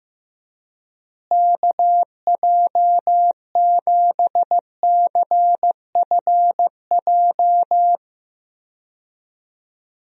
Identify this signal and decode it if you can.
Morse code, heard as a keyed tone, about 15 wpm, 700 Hz. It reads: KJ7CFJ
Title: USA Call Signs - 6 Characters - Repeated 2x - 15wpm